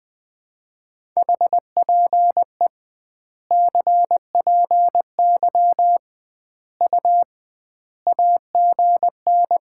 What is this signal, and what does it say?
Morse code, heard as a keyed tone, about 20 wpm, 700 Hz. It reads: HPE CPY U AGN
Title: Common QSO Phrases - 2x Word-Spacing - 20wpm